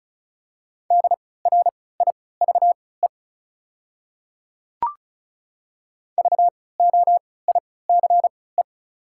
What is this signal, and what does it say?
Morse code, heard as a keyed tone, about 34 wpm, spaced as for 20 wpm, 700 Hz. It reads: DRIVE VOICE
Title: 3x Character Spacing - Top 500 Words - 35wpm